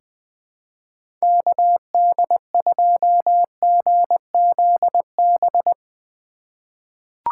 Morse code, heard as a keyed tone, about 20 wpm, 700 Hz. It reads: KD2GZB
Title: USA Call Signs - 6 Characters - 4x Character Spacing - 20wpm